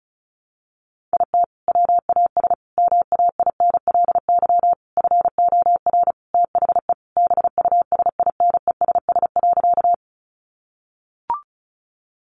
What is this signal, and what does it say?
Morse code, heard as a keyed tone, about 34 wpm, 700 Hz. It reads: IT WAS MAINLY FOR THE BUSINESS.
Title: Sentences from Top 1400 Words 35wpm